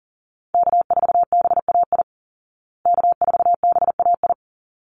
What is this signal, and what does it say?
Morse code, heard as a keyed tone, about 37 wpm, 700 Hz. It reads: K4BAI K4BAI
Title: Call Signs Repeated 3x - 40wpm